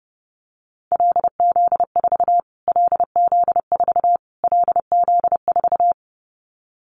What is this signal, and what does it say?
Morse code, heard as a keyed tone, about 30 wpm, 700 Hz. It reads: LZ4 LZ4 LZ4
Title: Callsign Prefixes for Top 25 DXCC - Repeated 3x - 30wpm